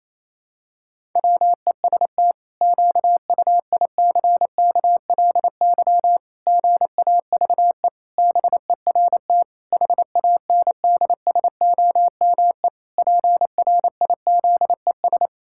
Morse code, heard as a keyed tone, about 28 wpm, 700 Hz. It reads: WEST QUICKLY GAVE BERT HANDSOME PRIZES
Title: Pangram Challenge v2 - 28wpm